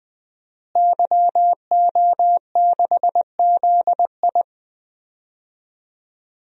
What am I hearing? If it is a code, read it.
Morse code, heard as a keyed tone, about 20 wpm, 700 Hz. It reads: YO6ZI
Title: DXCC-100 Call Signs - 5 Characters - Repeated 2x - 20wpm